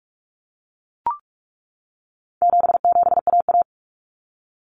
Morse code, heard as a keyed tone, about 44 wpm, 700 Hz. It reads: 77AA
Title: Common Cut Numbers - 4-Digit 45wpm